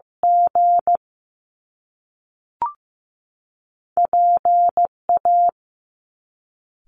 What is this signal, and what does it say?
Morse code, heard as a keyed tone, about 15 wpm, 700 Hz. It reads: G PA
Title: Top Two Letter Combinations in Words 15wpm